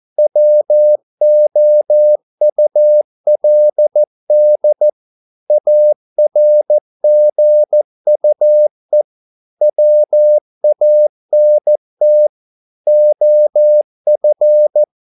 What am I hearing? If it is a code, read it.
Morse code, heard as a keyed tone, about 14 wpm, 600 Hz. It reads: WOULD ARGUE WANT OF
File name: mp3 files / a dogs mistake - 14wpm